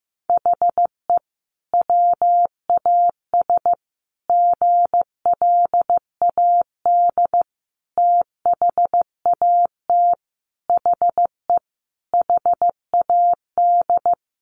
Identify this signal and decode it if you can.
Morse code, heard as a keyed tone, about 15 wpm, 700 Hz. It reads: HE WAS GLAD THAT HE HAD
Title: Sentences from Top 1400 Words 15wpm